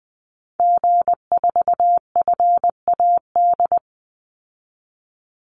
Morse code, heard as a keyed tone, about 20 wpm, 700 Hz. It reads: G4FAD
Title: Call Signs Repeated 2x - 20wpm